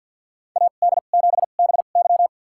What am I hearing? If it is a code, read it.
Morse code, heard as a keyed tone, about 47 wpm, spaced as for 35 wpm, 690 Hz. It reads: AD7BY
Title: USA Call Signs - 2x Character Spacing - 50wpm